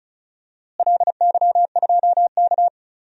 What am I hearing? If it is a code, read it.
Morse code, heard as a keyed tone, about 36 wpm, 700 Hz. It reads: LY2K
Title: DXCC-100 Call Signs - 4 Characters - Repeated 3x - 35wpm